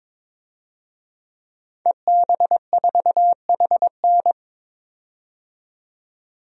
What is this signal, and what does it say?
Morse code, heard as a keyed tone, about 22 wpm, 700 Hz. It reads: EB4HN
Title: DXCC-100 Call Signs - 5 Characters - 22wpm